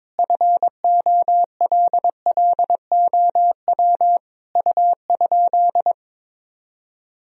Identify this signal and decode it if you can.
Morse code, heard as a keyed tone, about 22 wpm, 700 Hz. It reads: FOLLOW U?
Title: Sentences from Top 200 Words and Abbreviations 22wpm